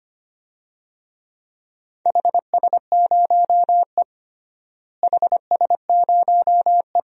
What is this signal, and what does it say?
Morse code, heard as a keyed tone, about 25 wpm, 700 Hz. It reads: HS0E HS0E
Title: DXCC-100 Call Signs - 4 Characters - Repeated 3x - 25wpm